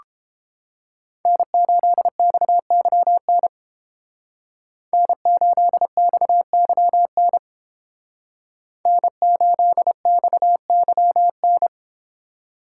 Morse code, heard as a keyed tone, about 31 wpm, 700 Hz. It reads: N8XYN N8XYN N8XYN